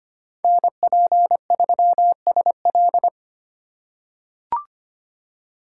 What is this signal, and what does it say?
Morse code, heard as a keyed tone, about 25 wpm, 700 Hz. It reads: NP3SL
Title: USA Call Signs - 5 Characters - 4x Character Spacing - 25wpm